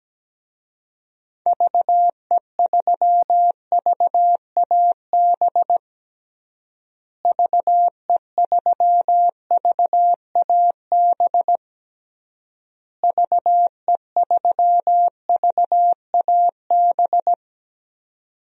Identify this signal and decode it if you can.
Morse code, heard as a keyed tone, about 17 wpm, 700 Hz. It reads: VE3VAB VE3VAB VE3VAB
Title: DXCC-100 Call Signs - 6 Characters - Repeated 3x - 17wpm